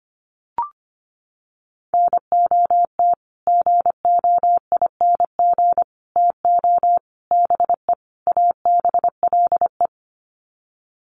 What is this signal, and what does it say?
Morse code, heard as a keyed tone, about 25 wpm, 700 Hz. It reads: NOT GOING TO BE ABLE
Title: Sets of 5 Words 25wpm